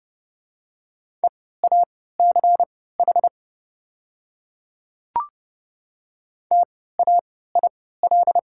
Morse code, heard as a keyed tone, about 30 wpm, spaced as for 17 wpm, 700 Hz. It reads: EACH TAIL
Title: 3x Character Spacing - Top 500 Words - 30wpm